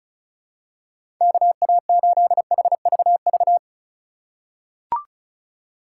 Morse code, heard as a keyed tone, about 36 wpm, 700 Hz. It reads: KA8HVV
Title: USA Call Signs - 6 Characters - 4x Character Spacing - 35wpm